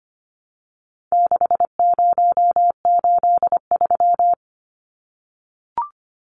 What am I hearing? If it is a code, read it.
Morse code, heard as a keyed tone, about 25 wpm, 700 Hz. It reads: 6083
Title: Numbers - 4-Digit 25wpm